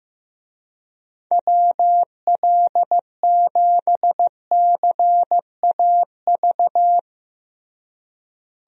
Morse code, heard as a keyed tone, about 15 wpm, 700 Hz. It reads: WL7CAV
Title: USA Call Signs - 6 Characters - Repeated 3x - 15wpm